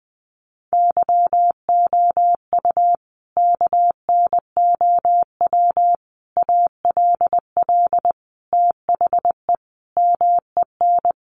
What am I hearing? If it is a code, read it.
Morse code, heard as a keyed tone, about 20 wpm, 700 Hz. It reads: YOU KNOW ALL THE MEN
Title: Sentences from Top 300 Words 20wpm